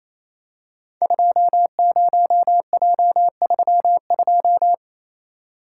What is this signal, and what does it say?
Morse code, heard as a keyed tone, about 27 wpm, 700 Hz. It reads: 20J32
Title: Code Groups - 5 Letter:Numbers - 28wpm